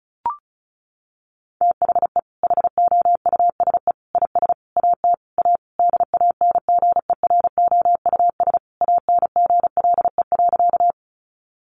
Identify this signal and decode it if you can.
Morse code, heard as a keyed tone, about 34 wpm, 700 Hz. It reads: THE HOUSE IS AT A DANGEROUS ANGLE.
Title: Sentences from Top 1500 Words 35wpm